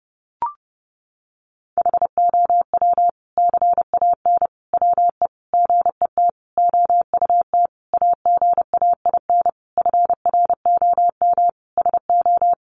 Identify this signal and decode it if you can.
Morse code, heard as a keyed tone, about 30 wpm, 700 Hz. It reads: HOW CAN WE GET OUT AGAIN FROM SO
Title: Sentences from Top 1200 Words 30wpm